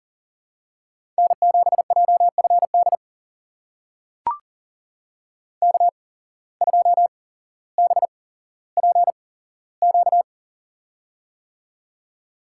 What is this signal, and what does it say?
Morse code, heard as a keyed tone, about 38 wpm, spaced as for 17 wpm, 700 Hz. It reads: N7JFD K2BPQ
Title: USA Call Signs - 5 Characters - 8x Character Spacing - 40wpm